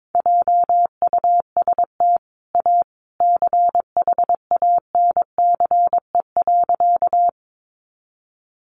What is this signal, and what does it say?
Morse code, heard as a keyed tone, about 22 wpm, 700 Hz. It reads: JUST A CHANCE.
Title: Sentences from Top 800 Words 22wpm